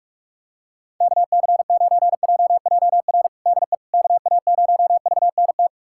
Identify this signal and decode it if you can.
Morse code, heard as a keyed tone, about 41 wpm, 700 Hz. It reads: KC9JJR DE KA0VNT
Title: USA Callsign DE Callsign - 6 Characters - 45wpm